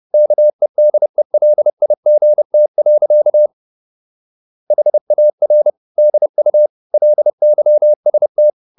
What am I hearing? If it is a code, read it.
Morse code, heard as a keyed tone, about 30 wpm, 600 Hz. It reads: KEDELIGT. HAR DU LYST
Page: Ferie 30 wpm